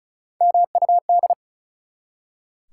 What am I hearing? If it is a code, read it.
Morse code, heard as a keyed tone, about 36 wpm, 700 Hz. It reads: MUD
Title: Three Letter Words - 35wpm